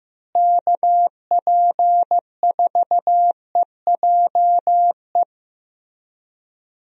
Morse code, heard as a keyed tone, about 15 wpm, 700 Hz. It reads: KP4EJE
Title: DXCC-100 Call Signs - 6 Characters - Repeated 3x - 15wpm